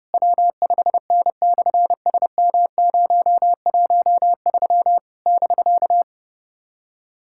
Morse code, heard as a keyed tone, about 30 wpm, 700 Hz. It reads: W5N/SM013 <BK>
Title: SOTA QSO Practice 30wpm